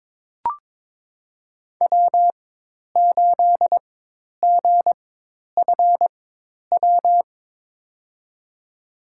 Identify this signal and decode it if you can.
Morse code, heard as a keyed tone, about 22 wpm, spaced as for 10 wpm, 700 Hz. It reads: W8GFW
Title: USA Call Signs - 5 Characters - 4x Character Spacing - 22wpm